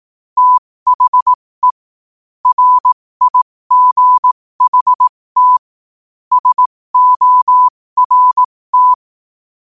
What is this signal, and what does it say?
Morse code, heard as a keyed tone, about 18 wpm, spaced as for 15 wpm, 1000 Hz. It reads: THE RIGHT SORT
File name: qotdfiles / 15wpm